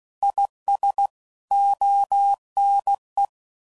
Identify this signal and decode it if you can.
Morse code, heard as a keyed tone, about 16 wpm, 800 Hz. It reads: IS ONE